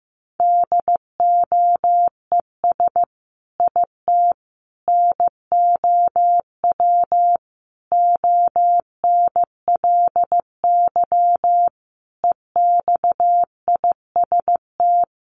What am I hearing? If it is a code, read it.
Morse code, heard as a keyed tone, about 15 wpm, 700 Hz. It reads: DOES IT NOW ONLY EXIST
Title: Sentences from Top 1300 Words 15wpm